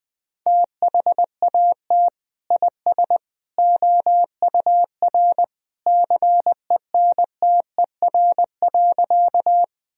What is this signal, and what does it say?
Morse code, heard as a keyed tone, about 20 wpm, 700 Hz. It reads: THAT IS OUR CENTER.
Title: Sentences from Top 1000 Words - Review Part 2 - 20wpm